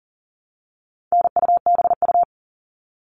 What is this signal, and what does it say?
Morse code, heard as a keyed tone, about 40 wpm, 700 Hz. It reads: NUBU
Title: All Cut Numbers - 4-Digit 40wpm